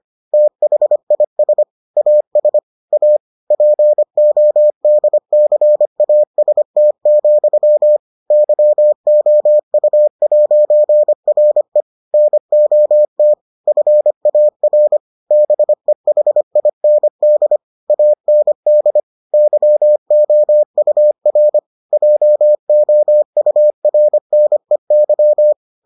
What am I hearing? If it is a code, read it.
Morse code, heard as a keyed tone, about 25 wpm, 600 Hz. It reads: THIS AS A PODCAST, YOU'RE NOT FAR BEHIND AND YOUR JOURNEY